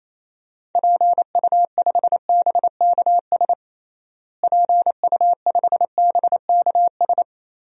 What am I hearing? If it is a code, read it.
Morse code, heard as a keyed tone, about 28 wpm, 700 Hz. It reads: PU5BKS PU5BKS
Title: DXCC-100 Call Signs - 6 Characters - Repeated 3x - 28wpm